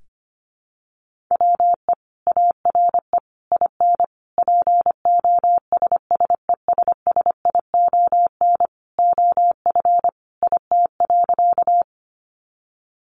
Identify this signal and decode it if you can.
Morse code, heard as a keyed tone, about 25 wpm, 700 Hz. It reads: WE ARE IN POSSESSION OF IT.
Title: Sentences from Top 1400 Words 25wpm